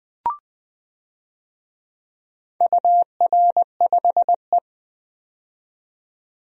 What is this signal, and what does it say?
Morse code, heard as a keyed tone, about 20 wpm, 700 Hz. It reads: UR5E
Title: DXCC-100 Call Signs - 4 Characters - 20wpm